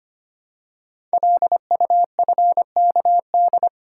Morse code, heard as a keyed tone, about 25 wpm, 700 Hz. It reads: LUFKD
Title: Code Groups - 5 Letters - 25wpm